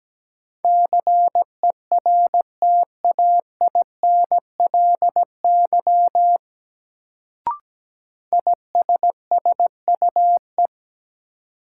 Morse code, heard as a keyed tone, about 17 wpm, 700 Hz. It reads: CERTAINLY ISSUE
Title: Sentences from Top 900 Words - New Word Drill - 17wpm